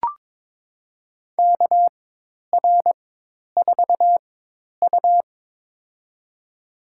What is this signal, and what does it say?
Morse code, heard as a keyed tone, about 22 wpm, spaced as for 10 wpm, 700 Hz. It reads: KR4U
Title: USA Call Signs - 4 Characters - 4x Character Spacing - 22wpm